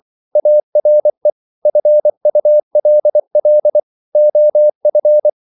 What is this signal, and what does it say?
Morse code, heard as a keyed tone, about 24 wpm, 600 Hz. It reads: ARE FULL OF
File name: CW / Practice / EnglishQuotations0324